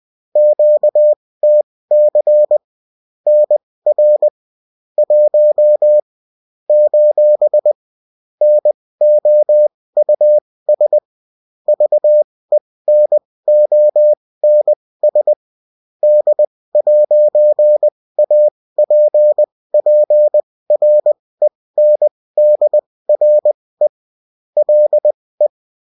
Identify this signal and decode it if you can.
Morse code, heard as a keyed tone, about 20 wpm, spaced as for 16 wpm, 600 Hz. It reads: QTC NR 1 : NOUS VENONS D'APPRENDRE LE